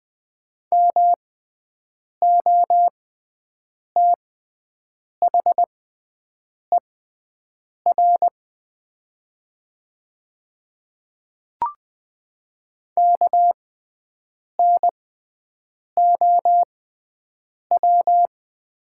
Morse code, heard as a keyed tone, about 20 wpm, spaced as for 7 wpm, 700 Hz. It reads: MOTHER KNOW
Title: 6x Character Spacing - Top 200 Words - 20wpm